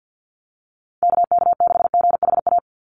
Morse code, heard as a keyed tone, about 48 wpm, 700 Hz. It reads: KK6GHA
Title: Call Signs - 50wpm